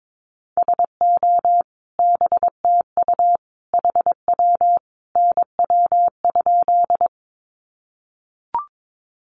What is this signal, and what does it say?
Morse code, heard as a keyed tone, about 22 wpm, 700 Hz. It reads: SO BTU HW NW?
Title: Common QSO Phrases 22wpm